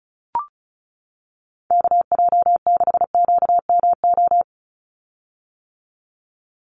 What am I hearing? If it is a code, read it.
Morse code, heard as a keyed tone, about 34 wpm, 700 Hz. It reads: KJ6QMO
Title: USA Call Signs 35wpm